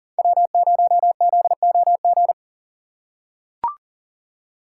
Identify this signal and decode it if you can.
Morse code, heard as a keyed tone, about 38 wpm, 700 Hz. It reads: W0ZOG
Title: USA Call Signs - 5 Characters - 4x Character Spacing - 40wpm